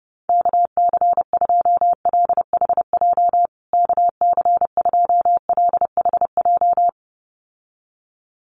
Morse code, heard as a keyed tone, about 30 wpm, 700 Hz. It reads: KC2LHJ KC2LHJ
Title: USA Call Signs Repeated 2x 30wpm